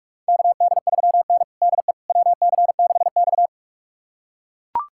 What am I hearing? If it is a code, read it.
Morse code, heard as a keyed tone, about 42 wpm, 700 Hz. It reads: KD3N DE WC6X
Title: USA Callsign DE Callsign - 4 Characters - 45wpm